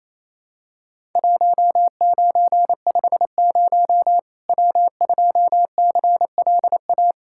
Morse code, heard as a keyed tone, about 28 wpm, 700 Hz. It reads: 1950 W2CLA
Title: Worked All Europe DX Contest - QTC Exchange - 28wpm